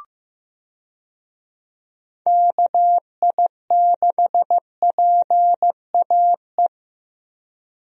Morse code, heard as a keyed tone, about 15 wpm, 700 Hz. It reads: KI6PAE
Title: USA Call Signs - 6 Characters - 15wpm